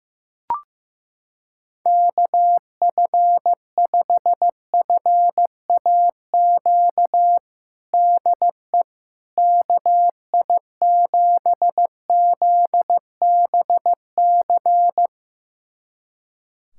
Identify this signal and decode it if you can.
Morse code, heard as a keyed tone, about 16 wpm, 700 Hz. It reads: KF5FAQ DE KI7ZBC
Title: USA Callsign DE Callsign 15wpm